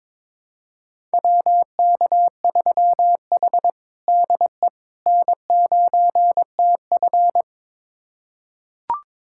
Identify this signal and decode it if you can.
Morse code, heard as a keyed tone, about 22 wpm, 700 Hz. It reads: WK3H DE N9TF
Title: USA Callsign DE Callsign - 4 Characters - 22wpm